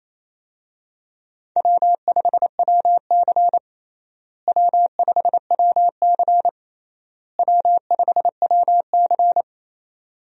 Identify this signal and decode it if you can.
Morse code, heard as a keyed tone, about 28 wpm, 700 Hz. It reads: W5WC W5WC W5WC
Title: USA Call Signs - 4 Characters - Repeated 3x - 28wpm